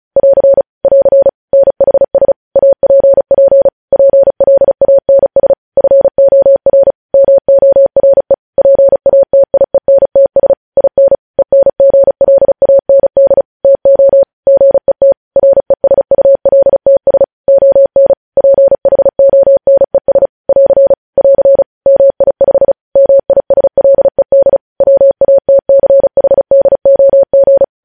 Morse code, heard as a keyed tone, about 36 wpm, 550 Hz. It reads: <AR> <AR> NHS APP PLANS FOR MORE PATIENTS IN ENGLAND TO GET RESULTS ON PHONES <AR> <AR> MI5 MISLED WATCHDOG
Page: news headlines in morse code at 35 wpm